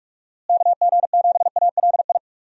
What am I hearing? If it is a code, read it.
Morse code, heard as a keyed tone, about 46 wpm, 700 Hz. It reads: KG7ALI
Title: USA Call Signs - 6 Characters - Repeated 2x - 45wpm